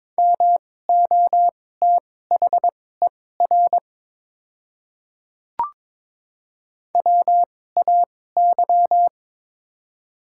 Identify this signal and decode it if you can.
Morse code, heard as a keyed tone, about 22 wpm, spaced as for 16 wpm, 700 Hz. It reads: MOTHER WAY
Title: 2x Character Spacing - Top 200 Words - 22wpm